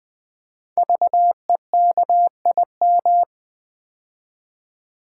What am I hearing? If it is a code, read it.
Morse code, heard as a keyed tone, about 20 wpm, 700 Hz. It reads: VEKIM
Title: Code Groups - 5 Letters - 20wpm